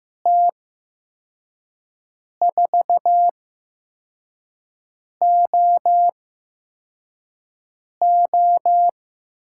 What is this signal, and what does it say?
Morse code, heard as a keyed tone, about 15 wpm, spaced as for 4 wpm, 700 Hz. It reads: T4OO
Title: USA Call Signs - 5 Characters - 8x Character Spacing - 15wpm